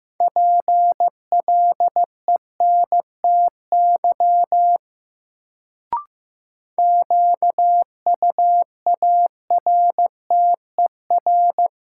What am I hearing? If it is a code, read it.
Morse code, heard as a keyed tone, about 15 wpm, 700 Hz. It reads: PLENTY QUARTER
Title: Sentences from Top 1300 Words - New Word Drill - 15wpm